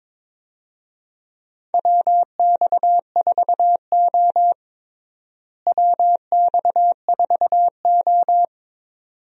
Morse code, heard as a keyed tone, about 22 wpm, 700 Hz. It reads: WX4O WX4O
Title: USA Call Signs - 4 Characters - Repeated 2x - 22wpm